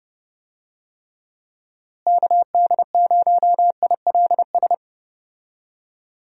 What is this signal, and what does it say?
Morse code, heard as a keyed tone, about 30 wpm, 700 Hz. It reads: KD0ILS
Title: USA Call Signs - 6 Characters - Repeated 3x - 30wpm